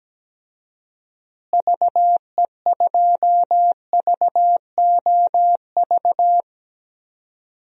Morse code, heard as a keyed tone, about 17 wpm, 700 Hz. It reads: VE2VOV
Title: DXCC-100 Call Signs - 6 Characters - Repeated 3x - 17wpm